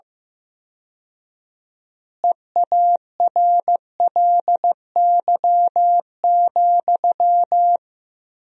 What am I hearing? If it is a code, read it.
Morse code, heard as a keyed tone, about 15 wpm, 700 Hz. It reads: EARLY,
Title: 4x Word-Spacing - Sentences from Top 400 Words - 15wpm